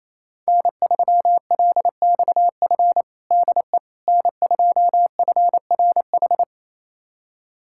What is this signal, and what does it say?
Morse code, heard as a keyed tone, about 28 wpm, 700 Hz. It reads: N3LXF DE N2FRH
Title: USA Callsign DE Callsign - 5 Characters - 28wpm